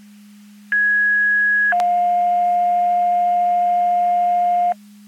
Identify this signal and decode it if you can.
Two-tone page, heard as a one-second tone, then a three-second tone, about 1650 then 710 Hz, with noise. It is heard from 0.7 to 4.7 s.
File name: Portage 154.385 Paging